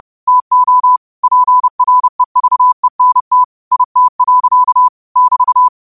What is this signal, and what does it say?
Morse code, heard as a keyed tone, about 30 wpm, 1000 Hz. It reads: TO PREVENT IT. <BT>